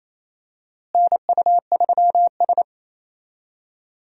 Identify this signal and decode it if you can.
Morse code, heard as a keyed tone, about 28 wpm, 700 Hz. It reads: NU3S
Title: USA Call Signs - 4 Characters - Repeated 3x - 28wpm